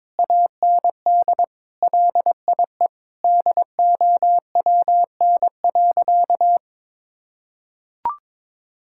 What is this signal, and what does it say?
Morse code, heard as a keyed tone, about 22 wpm, 700 Hz. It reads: AND LIE DOWN.
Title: Sentences from Top 1600 Words 22wpm